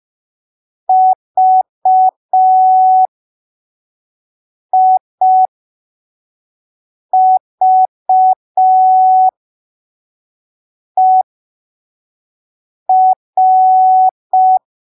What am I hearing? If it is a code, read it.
Morse code, heard as a keyed tone, about 5 wpm, spaced as for 3 wpm, 750 Hz. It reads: VIVER
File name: arquivos / texto / texto 15 05ppm